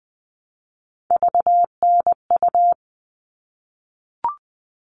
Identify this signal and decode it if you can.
Morse code, heard as a keyed tone, about 20 wpm, 700 Hz. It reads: VNU
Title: All Cut Numbers - 3-Digit 20wpm